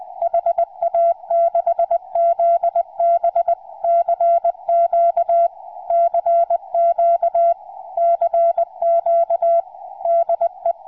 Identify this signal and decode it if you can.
Morse code, heard as a keyed tone, about 20 wpm, 680 Hz. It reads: HA6ZB CQ CQ CQ DE
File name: ha6zb 2m